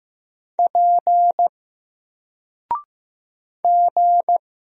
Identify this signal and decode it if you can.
Morse code, heard as a keyed tone, about 15 wpm, 700 Hz. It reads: P G
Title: Single Letter-Number 15wpm